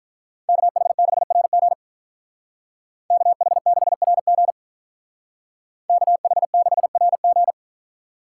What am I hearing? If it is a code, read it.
Morse code, heard as a keyed tone, about 43 wpm, 700 Hz. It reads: KH6RG KH6RG KH6RG
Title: Speed Racing - USA Call Signs 35wpm